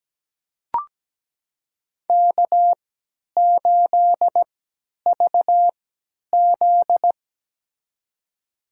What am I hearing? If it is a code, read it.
Morse code, heard as a keyed tone, about 17 wpm, spaced as for 10 wpm, 700 Hz. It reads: K8VZ
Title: USA Call Signs - 4 Characters - 3x Character Spacing - 17wpm